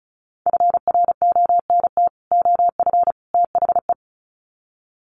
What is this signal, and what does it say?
Morse code, heard as a keyed tone, about 34 wpm, 700 Hz. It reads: FRONT OF THE
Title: Sets of 3 Words 35wpm